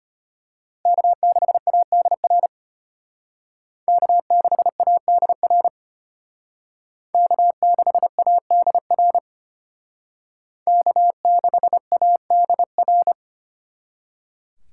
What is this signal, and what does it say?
Morse code, heard as a keyed tone, about 32 wpm, 700 Hz. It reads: K6ADR K6ADR K6ADR K6ADR